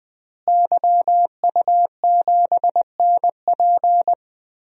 Morse code, heard as a keyed tone, about 20 wpm, 700 Hz. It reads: YU7NP
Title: DXCC-100 Call Signs - 5 Characters - Repeated 2x - 20wpm